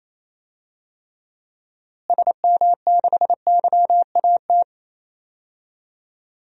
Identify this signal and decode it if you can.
Morse code, heard as a keyed tone, about 27 wpm, 700 Hz. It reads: SM6YAT
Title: DXCC-100 Call Signs - 6 Characters - Repeated 2x - 28wpm